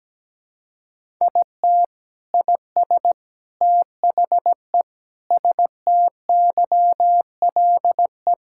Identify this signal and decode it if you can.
Morse code, heard as a keyed tone, about 17 wpm, 700 Hz. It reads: IT IS THE STYLE
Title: Sentences from Top 1000 Words - Review Part 5 - 17wpm